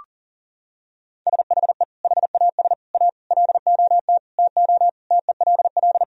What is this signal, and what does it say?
Morse code, heard as a keyed tone, about 40 wpm, 700 Hz. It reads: SHE HAS A LOT TO TELL
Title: Sentences from Top 1000 Words - Review Part 3 - 40wpm